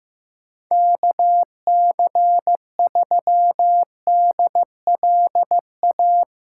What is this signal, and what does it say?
Morse code, heard as a keyed tone, about 15 wpm, 700 Hz. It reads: KC3DLA
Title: USA Call Signs - 6 Characters - 2x Character Spacing - 15wpm